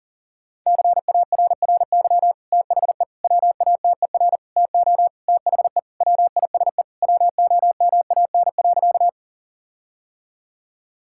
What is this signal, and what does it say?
Morse code, heard as a keyed tone, about 40 wpm, 700 Hz. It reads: CARRY THE WATER TO THE WISE WOMAN.
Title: Sentences from Top 1600 Words 40wpm